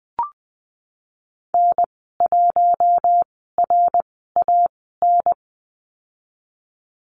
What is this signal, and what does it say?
Morse code, heard as a keyed tone, about 20 wpm, spaced as for 15 wpm, 700 Hz. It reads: N1RAN
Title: 076 Lesson 13g - Letter L - Call Signs 20wpm